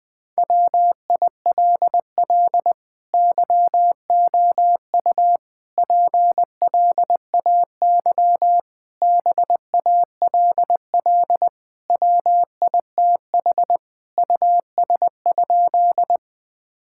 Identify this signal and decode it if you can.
Morse code, heard as a keyed tone, about 21 wpm, 700 Hz. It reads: WILL YOU PLAY BALL WITH US?
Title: Sentences from Top 1000 Words - Review Part 6 - 20wpm